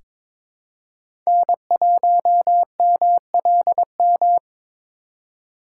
Morse code, heard as a keyed tone, about 22 wpm, 700 Hz. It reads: N1MLM
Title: USA Call Signs - 4x Character Spacing - 22wpm